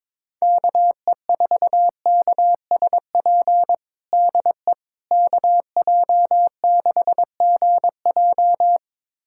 Morse code, heard as a keyed tone, about 22 wpm, 700 Hz. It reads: KE4KSP DE KJ6GJ
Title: USA Callsign DE Callsign 22wpm